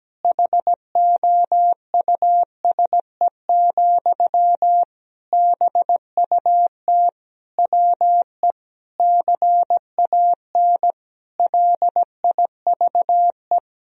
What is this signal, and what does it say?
Morse code, heard as a keyed tone, about 17 wpm, 700 Hz. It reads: HOUSE, BUT WE CAN LIVE
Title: Sentences from Top 1000 Words - Review Part 1 - 17wpm